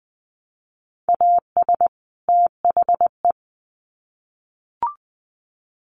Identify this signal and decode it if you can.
Morse code, heard as a keyed tone, about 20 wpm, 700 Hz. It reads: AS THE
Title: Sets of 2 Words 20wpm